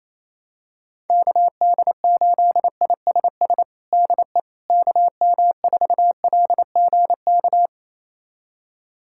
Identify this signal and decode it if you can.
Morse code, heard as a keyed tone, about 28 wpm, 700 Hz. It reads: KD8ISS DE KM4LGK
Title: USA Callsign DE Callsign - 6 Characters - 28wpm